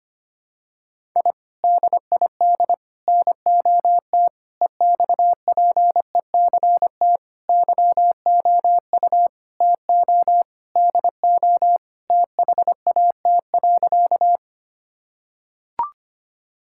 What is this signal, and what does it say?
Morse code, heard as a keyed tone, about 25 wpm, 700 Hz. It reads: I DID NOT EXPECT YOU TO DO THAT.
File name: practice / 25wpm / Sentences with 3-Word Phrases - 25wpm